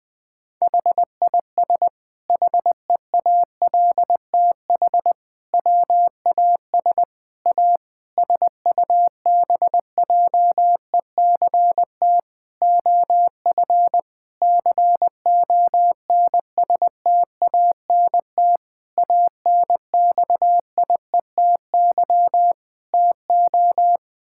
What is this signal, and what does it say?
Morse code, heard as a keyed tone, about 20 wpm, 700 Hz. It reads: HIS HEALTH WAS A SUBJECT OF CONSTANT ANXIETY TO
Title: Sentences from Top 1600 Words 20wpm